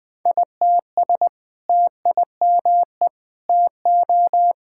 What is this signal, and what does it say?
Morse code, heard as a keyed tone, about 20 wpm, 700 Hz. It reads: ITS TIME TO
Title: Sentences with 2-Word Phrases - Spoken Beginning - 20wpm